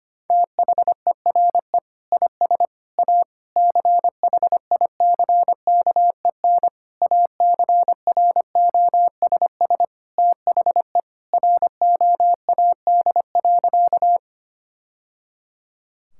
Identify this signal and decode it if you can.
Morse code, heard as a keyed tone, about 25 wpm, 700 Hz. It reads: THERE IS A CHICKEN ACROSS THE ROAD.
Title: Sentences from Top 1000 Words - Review Part 5 - 25wpm